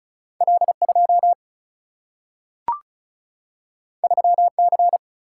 Morse code, heard as a keyed tone, about 36 wpm, 700 Hz. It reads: L2 3C
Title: Single-Letter Single-Number 35wpm